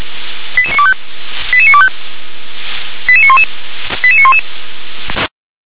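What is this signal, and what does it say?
Five-tone selective call (ZVEI), heard as quick runs of five tones, tones of about 80 ms. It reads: 88925 88925 88911 88911
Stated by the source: Five-tone ZVEI selective call received with an SDR.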